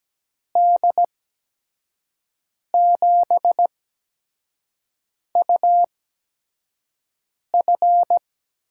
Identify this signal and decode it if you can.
Morse code, heard as a keyed tone, about 17 wpm, spaced as for 5 wpm, 700 Hz. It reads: D7UF
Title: USA Call Signs - 5 Characters - 8x Character Spacing - 17wpm